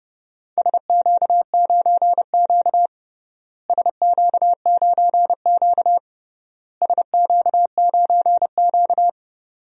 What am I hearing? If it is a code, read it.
Morse code, heard as a keyed tone, about 30 wpm, 700 Hz. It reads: SQ9Q SQ9Q SQ9Q
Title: DXCC-100 Call Signs - 4 Characters - Repeated 3x - 30wpm